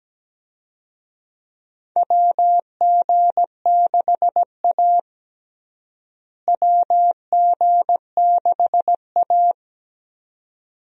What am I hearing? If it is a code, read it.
Morse code, heard as a keyed tone, about 17 wpm, 700 Hz. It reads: WG6A WG6A
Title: USA Call Signs - 4 Characters - Repeated 2x - 17wpm